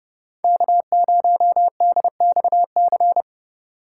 Morse code, heard as a keyed tone, about 30 wpm, 700 Hz. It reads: K0DXC
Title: K1USN SST QSO Practice - 30wpm